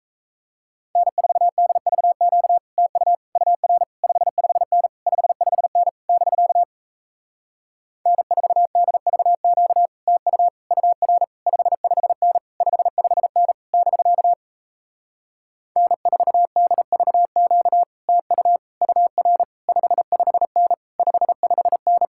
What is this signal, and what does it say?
Morse code, heard as a keyed tone, about 38 wpm, 700 Hz. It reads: N4DVQ TU UR 55N 55N <BK> N4DVQ TU UR 55N 55N <BK> N4DVQ TU UR 55N 55N